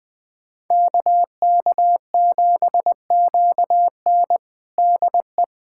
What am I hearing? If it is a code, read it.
Morse code, heard as a keyed tone, about 20 wpm, 700 Hz. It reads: KK7QN DE
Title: USA Callsign DE Callsign - 5 Characters - 20wpm